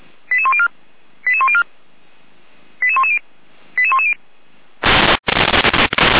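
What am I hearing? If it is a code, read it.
Five-tone selective call (ZVEI), heard as quick runs of five tones, tones of about 70 ms. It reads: 88104 88104 88110 88110
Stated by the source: Received on SDR as ZVEI five-tone selective call.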